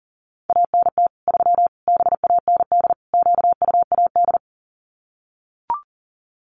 Morse code, heard as a keyed tone, about 40 wpm, 700 Hz. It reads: ANT 3 BAND QUAD
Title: Common QSO Phrases 40wpm